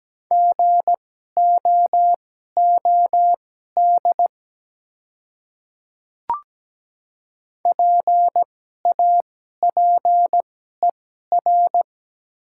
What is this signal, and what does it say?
Morse code, heard as a keyed tone, about 17 wpm, spaced as for 12 wpm, 700 Hz. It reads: GOOD PAPER
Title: 2x Character Spacing - Top 300 Words - 17wpm